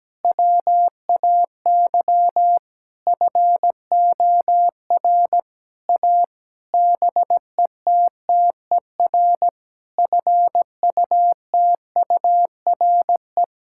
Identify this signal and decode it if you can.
Morse code, heard as a keyed tone, about 17 wpm, 700 Hz. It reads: WAY FOR A BETTER FUTURE
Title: Sentences from Top 1000 Words - Review Part 6 - 17wpm